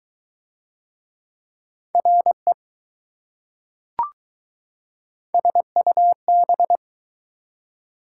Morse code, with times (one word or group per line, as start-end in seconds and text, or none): 1.9-2.5 RE
5.3-6.8 SUB